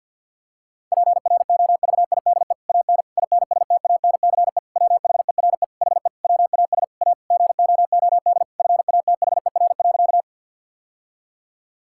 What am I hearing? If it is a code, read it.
Morse code, heard as a keyed tone, about 50 wpm, 700 Hz. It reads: PROVIDE AN INSTANCE WHERE HE WAS A GOOD FATHER.
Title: Sentences from Top 1000 Words - Review Part 5 - 50wpm